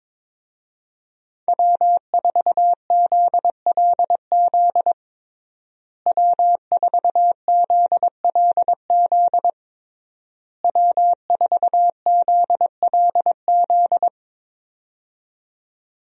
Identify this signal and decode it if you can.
Morse code, heard as a keyed tone, about 22 wpm, 700 Hz. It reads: W4ZLZ W4ZLZ W4ZLZ
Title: USA Call Signs - 5 Characters - Repeated 3x - 22wpm